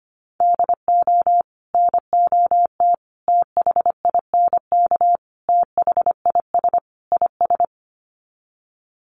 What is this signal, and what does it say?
Morse code, heard as a keyed tone, about 25 wpm, 700 Hz. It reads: DO NOT THINK THIS IS
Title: Sets of 5 Words Encore2 25wpm